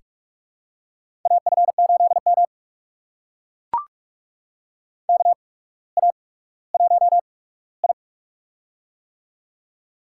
Morse code, heard as a keyed tone, about 44 wpm, spaced as for 19 wpm, 690 Hz. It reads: AF8M KA1I
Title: USA Call Signs - 4 Characters - 8x Character Spacing - 45wpm